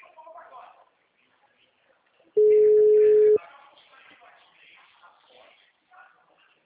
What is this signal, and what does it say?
Ringback tone (Europe, 425 Hz), with speech.
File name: out- -2019- - - .930